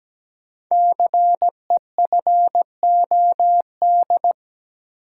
Morse code, heard as a keyed tone, about 17 wpm, 700 Hz. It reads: CEFOD
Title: Code Groups - 5 Letters - 17wpm